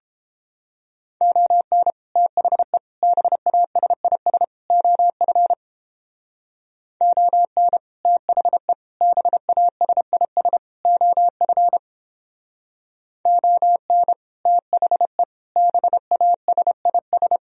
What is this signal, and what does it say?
Morse code, heard as a keyed tone, about 31 wpm, 700 Hz. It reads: ON THE BASIS OF ON THE BASIS OF ON THE BASIS